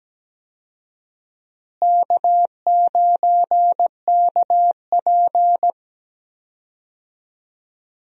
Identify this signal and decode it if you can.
Morse code, heard as a keyed tone, about 17 wpm, 700 Hz. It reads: K9KP
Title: USA Call Signs - 4 Characters - Repeated 2x - 17wpm